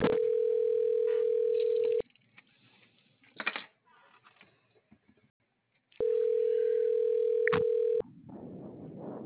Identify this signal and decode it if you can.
Ringback tone (North America), 440 + 480 Hz.